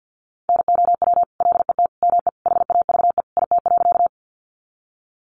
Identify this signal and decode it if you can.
Morse code, heard as a keyed tone, about 50 wpm, 700 Hz. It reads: NOW LET ME HAVE IT.
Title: Sentences from Top 300 Words 50wpm